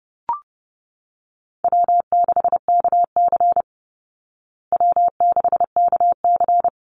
Morse code, heard as a keyed tone, about 30 wpm, 700 Hz. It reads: W6KC W6KC
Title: Easy Call Signs Repeated 3x - 30wpm